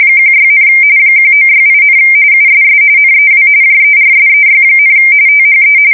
RTTY, about 45.45 baud. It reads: 16, 1968, USS NEW JERSEY, THE
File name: fsk170 / battleshipnj